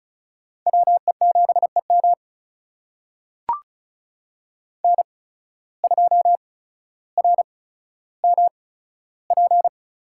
Morse code, heard as a keyed tone, about 36 wpm, spaced as for 15 wpm, 700 Hz. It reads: WE7EM N2RMP
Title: USA Call Signs - 8x Character Spacing - 35wpm